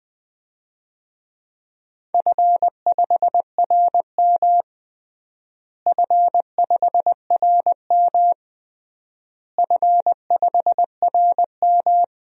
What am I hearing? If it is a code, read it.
Morse code, heard as a keyed tone, about 20 wpm, 700 Hz. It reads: F5RM F5RM F5RM
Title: DXCC-100 Call Signs - 4 Characters - Repeated 3x - 20wpm